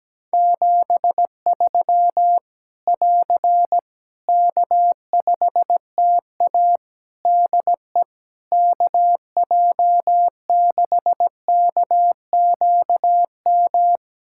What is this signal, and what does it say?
Morse code, heard as a keyed tone, about 17 wpm, 700 Hz. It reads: 73 <AR> K5TA DE KJ6KQM
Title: Practice QSO - Exchange 4 - Ending - 17wpm